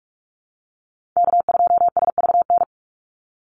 Morse code, heard as a keyed tone, about 44 wpm, 700 Hz. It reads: K2SVN
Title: USA Call Signs 45wpm